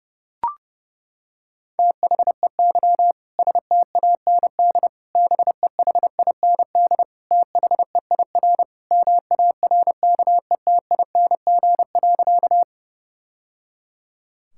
Morse code, heard as a keyed tone, about 30 wpm, 700 Hz. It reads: THEY STAND BEHIND THEIR MARKETING.
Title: Sentences from Top 1000 Words - Review Part 5 - 30wpm